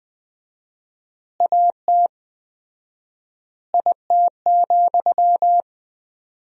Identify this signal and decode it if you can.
Morse code, heard as a keyed tone, about 20 wpm, 700 Hz. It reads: AT IT,
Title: 4x Word-Spacing - Sentences from Top 200 Words - 20wpm